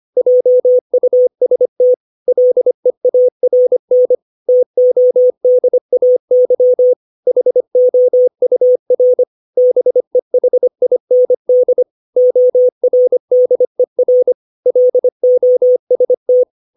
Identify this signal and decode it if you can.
Morse code, heard as a keyed tone, about 25 wpm, 500 Hz. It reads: JUST LEARN TODAY HOUR BEHIND ORDER LOST
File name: 632 words / 632 Words 25WPM